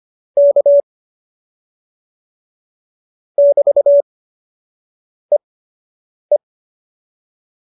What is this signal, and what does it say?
Morse code, heard as a keyed tone, about 25 wpm, spaced as for 7 wpm, 570 Hz. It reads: K <BT>EE